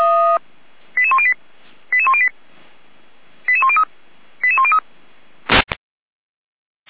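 Five-tone selective call (ZVEI), heard as quick runs of five tones, tones of about 70 ms. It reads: 88108 88108 88103 88103
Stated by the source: ZVEI five-tone selective call 88102 received via SDR.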